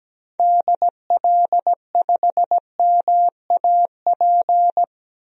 Morse code, heard as a keyed tone, about 17 wpm, 700 Hz. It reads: DL5MAP
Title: DXCC-100 Call Signs - 6 Characters - 17wpm